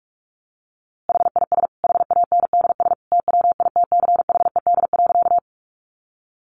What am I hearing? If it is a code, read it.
Morse code, heard as a keyed tone, about 44 wpm, 700 Hz. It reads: HIS HANDS TWITCHED.
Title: A Sound Of Thunder 45wpm